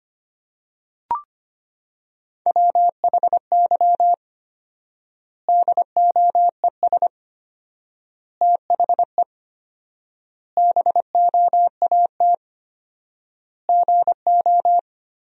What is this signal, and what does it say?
Morse code, heard as a keyed tone, about 25 wpm, 700 Hz. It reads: WHY DOES THE BOAT GO
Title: 4x Word-Spacing - Sentences from Top 500 Words - 25wpm